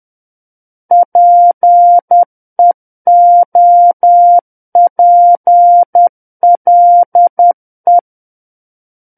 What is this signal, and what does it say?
Morse code, heard as a keyed tone, about 10 wpm, 700 Hz. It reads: PEOPLE
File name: Book One / 01-Chapter 1 - 10 WPM